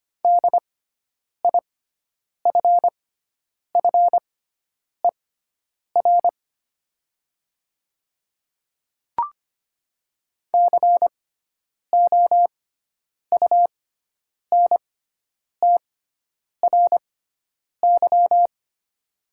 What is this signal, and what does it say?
Morse code, heard as a keyed tone, about 25 wpm, spaced as for 9 wpm, 700 Hz. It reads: DIFFER COUNTRY
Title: 6x Character Spacing - Top 500 Words - 25wpm